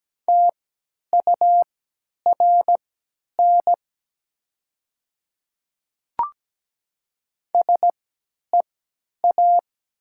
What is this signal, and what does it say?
Morse code, heard as a keyed tone, about 17 wpm, spaced as for 10 wpm, 700 Hz. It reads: TURN SEA
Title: 3x Character Spacing - Top 300 Words - 17wpm